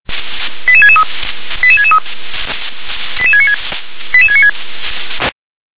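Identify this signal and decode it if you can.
Five-tone selective call (ZVEI), heard as quick runs of five tones, tones of about 70 ms. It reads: 88663 88663 88696 88696
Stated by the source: ZVEI five-tone call received by SDR.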